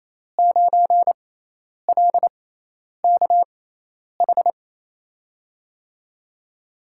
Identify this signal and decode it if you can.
Morse code, heard as a keyed tone, about 28 wpm, spaced as for 10 wpm, 700 Hz. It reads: 9LKH
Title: USA Call Signs - 5 Characters - 6x Character Spacing - 28wpm